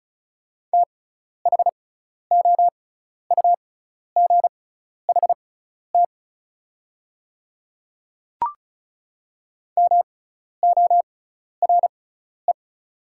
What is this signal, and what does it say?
Morse code, heard as a keyed tone, about 34 wpm, spaced as for 12 wpm, 700 Hz. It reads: THOUGHT MORE
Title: 6x Character Spacing - Top 500 Words - 35wpm